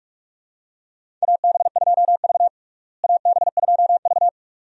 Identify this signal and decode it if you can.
Morse code, heard as a keyed tone, about 42 wpm, 700 Hz. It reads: AB2V AB2V
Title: USA Call Signs - 4 Characters - Repeated 2x - 45wpm